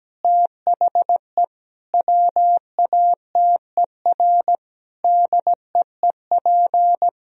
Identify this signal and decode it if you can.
Morse code, heard as a keyed tone, about 17 wpm, 700 Hz. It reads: THE WATER DEEP
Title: Sentences from Top 1000 Words - Review Part 4 - 17wpm